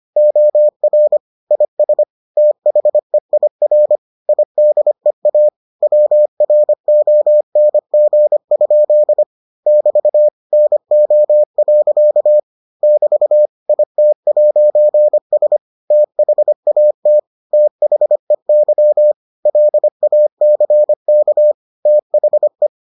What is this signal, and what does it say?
Morse code, heard as a keyed tone, about 25 wpm, 600 Hz. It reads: OR IS THEIR IDEA WRONG? <BT> NO. <BT> IT'S THAT THEY LACK THE